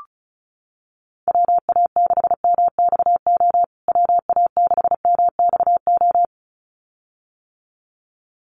Morse code, heard as a keyed tone, about 34 wpm, 700 Hz. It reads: WA6MXO WA6MXO
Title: USA Call Signs Repeated 2x 35wpm